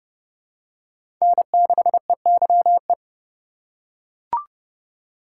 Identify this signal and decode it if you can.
Morse code, heard as a keyed tone, about 30 wpm, 700 Hz. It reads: N6EYE
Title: USA Call Signs - 5 Characters - 8x Character Spacing - 30wpm